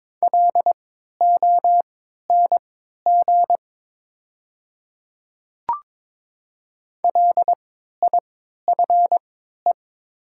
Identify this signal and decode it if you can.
Morse code, heard as a keyed tone, about 22 wpm, spaced as for 12 wpm, 700 Hz. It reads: LONG LIFE
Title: 3x Character Spacing - Top 300 Words - 22wpm